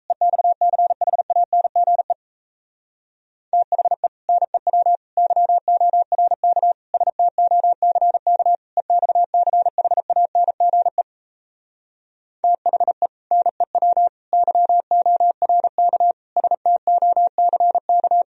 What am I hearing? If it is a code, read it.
Morse code, heard as a keyed tone, about 38 wpm, 700 Hz. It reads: EXCHANGE THE NEW YORK STOCK EXCHANGE THE NEW YORK STOCK